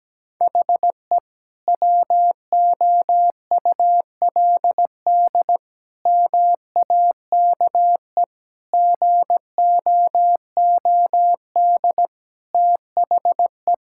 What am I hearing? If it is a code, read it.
Morse code, heard as a keyed tone, about 17 wpm, 700 Hz. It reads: HE WOULD MAKE GOOD THE
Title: Sentences from Top 1600 Words 17wpm